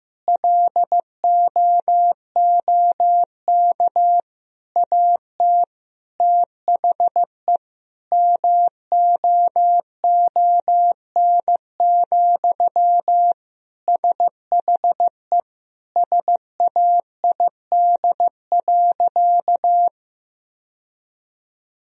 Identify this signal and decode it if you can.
Morse code, heard as a keyed tone, about 15 wpm, 700 Hz. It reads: LOOK AT THE MOON, SHE SAID.
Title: Sentences from Top 1000 Words - Review Part 3 - 15wpm